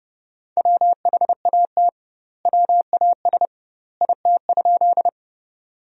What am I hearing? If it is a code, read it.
Morse code, heard as a keyed tone, about 30 wpm, 700 Hz. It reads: WHAT WAS IT?
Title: 2x Word-Spacing - Sentences from Top 100 Words - 30wpm